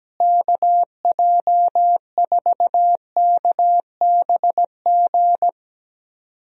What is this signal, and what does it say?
Morse code, heard as a keyed tone, about 17 wpm, 700 Hz. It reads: KJ4KBG
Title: USA Call Signs - 6 Characters - Repeated 2x - 17wpm